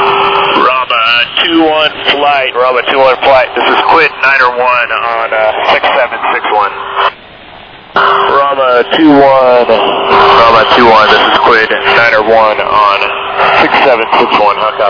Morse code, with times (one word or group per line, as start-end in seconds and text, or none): none